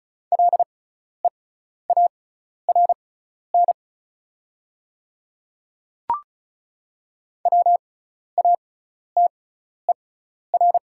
Morse code, heard as a keyed tone, about 36 wpm, spaced as for 12 wpm, 690 Hz. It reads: LEARN WATER
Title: 6x Character Spacing - Top 300 Words - 35wpm